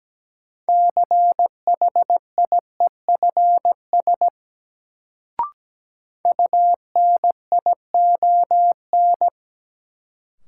Morse code, heard as a keyed tone, about 17 wpm, 700 Hz. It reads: CHIEFS UNION
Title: Sets of 5 Words Encore - New Word Drill - 17wpm